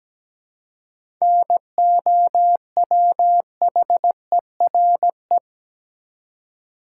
Morse code, heard as a keyed tone, about 17 wpm, 700 Hz. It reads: NOWHERE
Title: Seven Letter Words - 17wpm